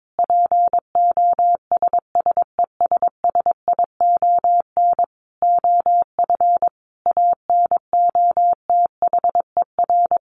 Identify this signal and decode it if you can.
Morse code, heard as a keyed tone, about 22 wpm, 700 Hz. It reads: POSSESSION OF ANOTHER
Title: Sentences from Top 1400 Words 22wpm